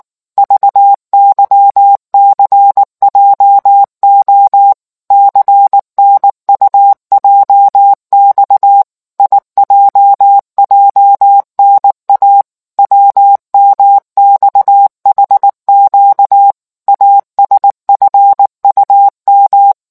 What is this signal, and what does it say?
Morse code, heard as a keyed tone, about 19 wpm, 800 Hz. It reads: VYCJO CNUJX IJJNA WMXHQ ASFUM